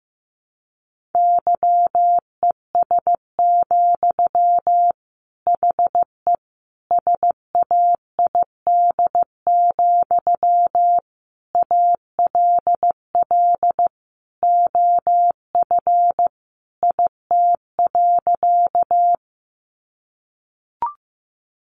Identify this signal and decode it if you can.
Morse code, heard as a keyed tone, about 15 wpm, 700 Hz. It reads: YES, HE SAID, ALL OF IT.
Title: Sentences from Top 500 Words 15wpm